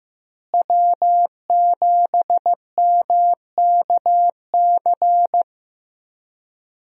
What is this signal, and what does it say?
Morse code, heard as a keyed tone, about 15 wpm, 700 Hz. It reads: W7MKC
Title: USA Call Signs - 5 Characters - Repeated 3x - 15wpm